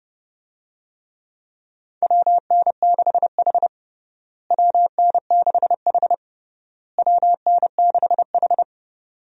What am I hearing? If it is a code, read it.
Morse code, heard as a keyed tone, about 32 wpm, 700 Hz. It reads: WN6H WN6H WN6H
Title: USA Call Signs - 4 Characters - Repeated 3x - 30wpm